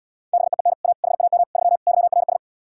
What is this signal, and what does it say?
Morse code, heard as a keyed tone, about 77 wpm, 700 Hz. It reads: BEEN A HAM 3 YRS